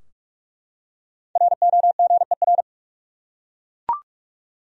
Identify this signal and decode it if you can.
Morse code, heard as a keyed tone, about 42 wpm, 700 Hz. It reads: ROGER
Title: USA Common Names - 45wpm